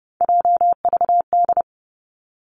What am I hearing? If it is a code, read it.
Morse code, heard as a keyed tone, about 30 wpm, 700 Hz. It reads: JVD
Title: Three Challenging Letter-Number Combinations 30wpm